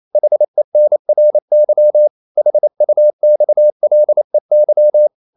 Morse code, heard as a keyed tone, about 28 wpm, 600 Hz. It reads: HENRY HUXLEY
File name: CW / Practice / EnglishQuotations0128